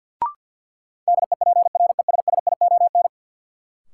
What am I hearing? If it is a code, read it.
Morse code, heard as a keyed tone, about 53 wpm, 700 Hz. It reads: DEPRESSION
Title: Sentences from Top 1300 Words - New Word Drill Alt - 50wpm